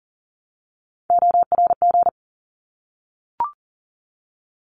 Morse code, heard as a keyed tone, about 40 wpm, 700 Hz. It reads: ORG
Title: ARRL Sections 40wpm